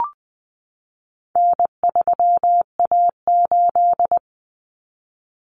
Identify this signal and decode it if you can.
Morse code, heard as a keyed tone, about 20 wpm, 700 Hz. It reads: N3A8
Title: Common Cut Numbers - 4-Digit 20wpm